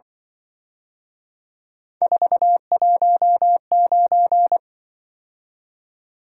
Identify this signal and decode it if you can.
Morse code, heard as a keyed tone, about 24 wpm, 700 Hz. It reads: 419
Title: Speed-Racing - Three Letter-Number Combinations - 20wpm